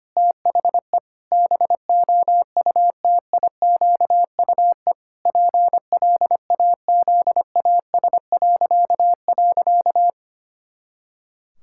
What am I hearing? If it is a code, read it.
Morse code, heard as a keyed tone, about 25 wpm, 700 Hz. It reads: THE BOUTIQUE PLAZAS..
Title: Pangram Challenge v2 - 25wpm